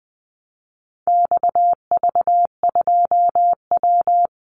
Morse code, heard as a keyed tone, about 20 wpm, 700 Hz. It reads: XV2W
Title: Call Signs Repeated 3x - 20wpm